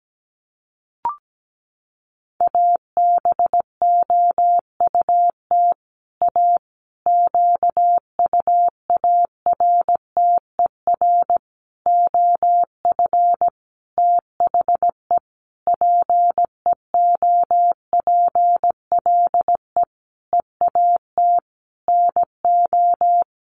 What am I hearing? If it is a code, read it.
Morse code, heard as a keyed tone, about 17 wpm, 700 Hz. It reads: ABOUT A QUARTER OF THE PEOPLE EAT NO
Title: Sentences from Top 1300 Words 17wpm